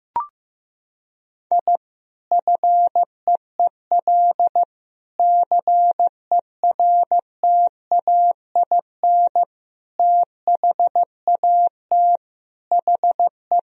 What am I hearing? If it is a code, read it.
Morse code, heard as a keyed tone, about 15 wpm, 700 Hz. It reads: I FEEL CERTAIN THAT HE
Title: Sentences from Top 1000 Words - Review Part 2 - 15wpm